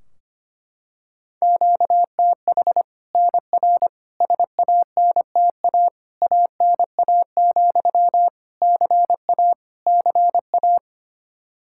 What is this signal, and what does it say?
Morse code, heard as a keyed tone, about 25 wpm, 700 Hz. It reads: QTH NR SANTA ANA, CA CA
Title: Practice QSO - Exchange 1 - Beginning - 25wpm